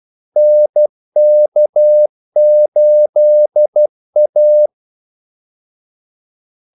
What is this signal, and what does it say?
Morse code, heard as a keyed tone, about 12 wpm, 600 Hz. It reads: NK8A
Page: 2 ekstra ordmellemrum mellem kaldesignalerne